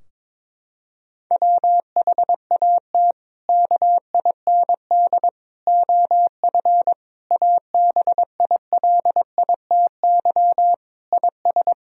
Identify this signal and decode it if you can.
Morse code, heard as a keyed tone, about 22 wpm, 700 Hz. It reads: WHAT KIND OF ABILITY IS
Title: Sentences from Top 1000 Words - Review Part 3 - 22wpm